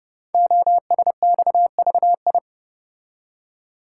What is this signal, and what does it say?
Morse code, heard as a keyed tone, about 30 wpm, 700 Hz. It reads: OSXVI
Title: Code Groups - 5 Letters - 30wpm